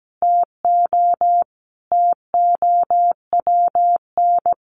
Morse code, heard as a keyed tone, about 17 wpm, 700 Hz. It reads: TO TOWN
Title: Sentences from Top 1500 Words 17wpm